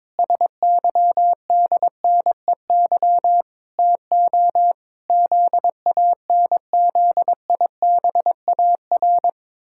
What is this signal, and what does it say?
Morse code, heard as a keyed tone, about 22 wpm, 700 Hz. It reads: SYDNEY TO ZANZIBAR
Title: Pangram Challenge v2 - 22wpm